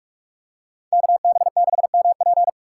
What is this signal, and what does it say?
Morse code, heard as a keyed tone, about 46 wpm, 700 Hz. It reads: KB6MP